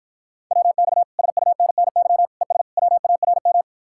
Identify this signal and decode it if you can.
Morse code, heard as a keyed tone, about 52 wpm, 700 Hz. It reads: WX SUNNY ES WARM